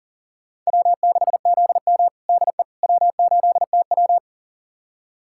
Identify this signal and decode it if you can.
Morse code, heard as a keyed tone, about 37 wpm, 700 Hz. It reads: W6ZM DE W8TW
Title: USA Callsign DE Callsign - 4 Characters - 40wpm